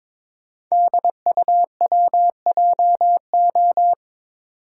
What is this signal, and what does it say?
Morse code, heard as a keyed tone, about 22 wpm, 700 Hz. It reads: DUWJO
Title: Code Groups - 5 Letter:Numbers - 22wpm